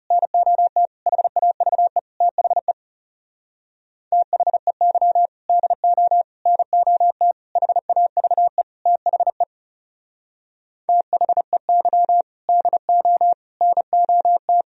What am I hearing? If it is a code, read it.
Morse code, heard as a keyed tone, about 40 wpm, 700 Hz. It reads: NOT HAVE THE THEY DO NOT HAVE THE THEY DO NOT